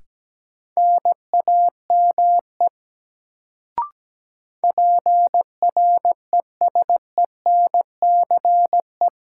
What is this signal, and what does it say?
Morse code, heard as a keyed tone, about 17 wpm, 700 Hz. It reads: NAME PRESENCE
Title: Sets of 3 Words - New Word Drill - 17wpm